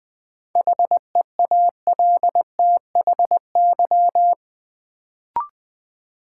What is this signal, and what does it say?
Morse code, heard as a keyed tone, about 20 wpm, 700 Hz. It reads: HEALTHY
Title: Sentences from Top 1000 Words - Review Part 5 - New Word Drill - 20wpm